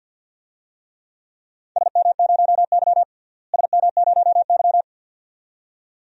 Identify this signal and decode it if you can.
Morse code, heard as a keyed tone, about 50 wpm, 700 Hz. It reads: SM0Y SM0Y
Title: DXCC-100 Call Signs - 4 Characters - Repeated 2x - 50wpm